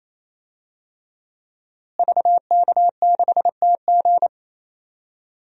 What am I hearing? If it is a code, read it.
Morse code, heard as a keyed tone, about 28 wpm, 700 Hz. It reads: VK6TG
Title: DXCC-100 Call Signs - 5 Characters - Repeated 3x - 28wpm